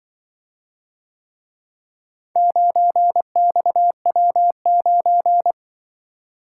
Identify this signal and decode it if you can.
Morse code, heard as a keyed tone, about 24 wpm, 700 Hz. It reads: 9XW9
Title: Speed-Racing - Four Letter-Number Combinations - 20wpm